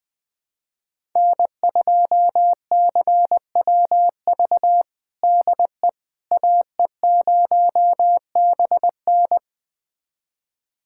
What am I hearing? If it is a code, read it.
Morse code, heard as a keyed tone, about 20 wpm, 700 Hz. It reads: N2CWV DE AE0BN
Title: USA Callsign DE Callsign - 5 Characters - 20wpm